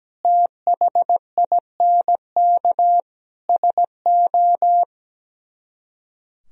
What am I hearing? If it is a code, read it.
Morse code, heard as a keyed tone, about 17 wpm, 700 Hz. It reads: THINK SO
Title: Speed-Racing - Sets of 5 Words 17wpm